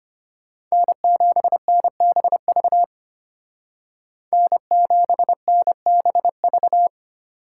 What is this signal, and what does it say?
Morse code, heard as a keyed tone, about 33 wpm, 700 Hz. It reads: N7NBV N7NBV